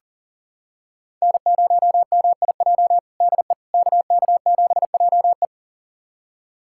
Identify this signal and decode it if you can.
Morse code, heard as a keyed tone, about 40 wpm, 700 Hz. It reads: N0MIJ DE KK7JE
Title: USA Callsign DE Callsign - 5 Characters - 40wpm